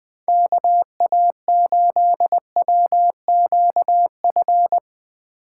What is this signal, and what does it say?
Morse code, heard as a keyed tone, about 20 wpm, 700 Hz. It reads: KA8WQF
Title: USA Call Signs - 6 Characters - 8x Character Spacing - 20wpm